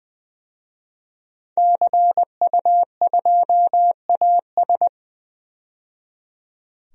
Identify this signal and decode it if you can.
Morse code, heard as a keyed tone, about 20 wpm, 700 Hz. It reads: CU2AS
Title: DXCC-100 Call Signs - 5 Characters - 20wpm